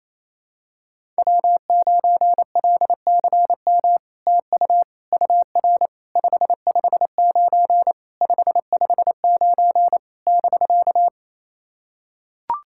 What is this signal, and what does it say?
Morse code, heard as a keyed tone, about 28 wpm, 700 Hz. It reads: W9LCM TU UR 559 559 <BK>
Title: SOTA QSO Practice 28wpm